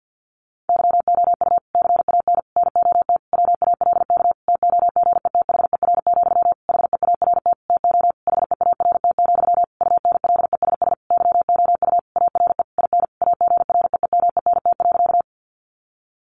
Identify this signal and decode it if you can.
Morse code, heard as a keyed tone, about 50 wpm, 700 Hz. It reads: YOU CAN NOT WALK TOGETHER, HEART TO HEART, UNLESS YOU ARE IN AGREEMENT.
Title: Sentences from Top 1500 Words 50wpm